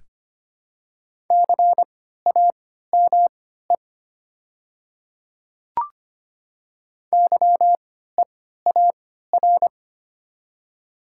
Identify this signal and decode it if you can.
Morse code, heard as a keyed tone, about 25 wpm, spaced as for 14 wpm, 700 Hz. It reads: CAME YEAR
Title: 3x Character Spacing - Top 500 Words - 25wpm